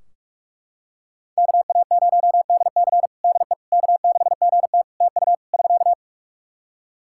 Morse code, heard as a keyed tone, about 41 wpm, 700 Hz. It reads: KA0DC DE K6GT TU <SK>
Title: Practice QSO - Exchange 4 - Ending - 45wpm